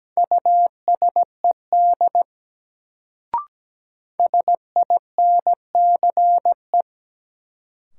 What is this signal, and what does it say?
Morse code, heard as a keyed tone, about 17 wpm, 700 Hz. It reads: USED SINCE
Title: Sets of 2 Words - New Word Drill - 17wpm